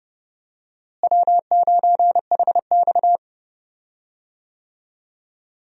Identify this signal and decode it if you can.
Morse code, heard as a keyed tone, about 30 wpm, 700 Hz. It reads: W9HX
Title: USA Call Signs - 4 Characters - Repeated 2x - 30wpm